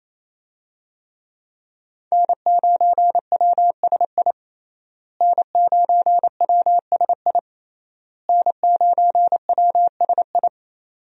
Morse code, heard as a keyed tone, about 28 wpm, 700 Hz. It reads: N9WSI N9WSI N9WSI
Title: USA Call Signs - 5 Characters - Repeated 3x - 28wpm